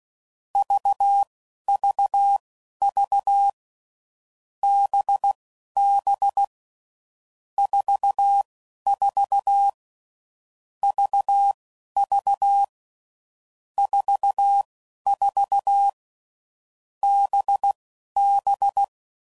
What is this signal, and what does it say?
Morse code, heard as a keyed tone, about 16 wpm, spaced as for 11 wpm, 800 Hz. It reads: VVV BB 44 VV 44 BB
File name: Groups / Basic Letter Groups / VB46